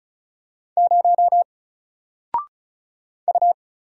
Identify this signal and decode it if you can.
Morse code, heard as a keyed tone, about 35 wpm, 700 Hz. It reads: 0 U
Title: Single Letter-Number 35wpm